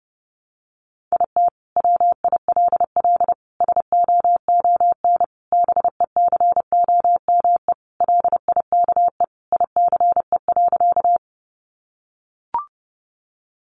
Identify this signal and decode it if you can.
Morse code, heard as a keyed tone, about 30 wpm, 700 Hz. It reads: IT WILL SOON BECOME LIKE ICE.
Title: Sentences from Top 1000 Words 30wpm